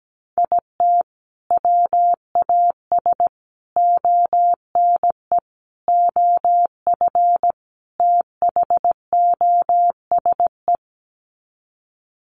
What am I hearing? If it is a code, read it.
Morse code, heard as a keyed tone, about 17 wpm, 700 Hz. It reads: IT WAS ONE OF THOSE
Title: Sets of 5 Words Encore 17wpm